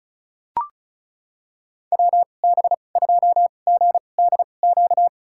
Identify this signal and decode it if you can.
Morse code, heard as a keyed tone, about 36 wpm, spaced as for 25 wpm, 700 Hz. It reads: WB2GDQ
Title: USA Call Signs - 2x Character Spacing - 35wpm